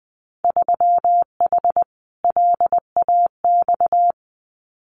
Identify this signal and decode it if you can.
Morse code, heard as a keyed tone, about 20 wpm, 700 Hz. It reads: 3H LAX
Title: Winter Field Day Exchanges 20wpm